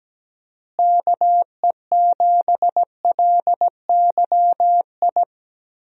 Morse code, heard as a keyed tone, about 17 wpm, 700 Hz. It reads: KE7LYI
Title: USA Call Signs - 6 Characters - Repeated 2x - 17wpm